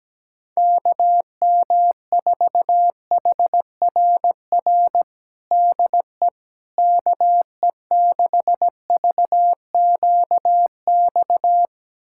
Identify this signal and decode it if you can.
Morse code, heard as a keyed tone, about 17 wpm, 700 Hz. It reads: KM4HRR DE KE6VQX
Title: USA Callsign DE Callsign 17wpm